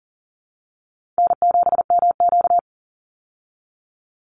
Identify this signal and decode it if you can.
Morse code, heard as a keyed tone, about 40 wpm, 700 Hz. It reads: N7MQ
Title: Easy Call Signs Repeated 2x - 40wpm